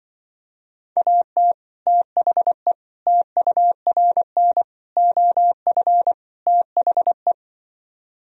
Morse code, heard as a keyed tone, about 24 wpm, 700 Hz. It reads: AT THE TURN OF THE
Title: Speed-Racing - Sets of 5 Words 20wpm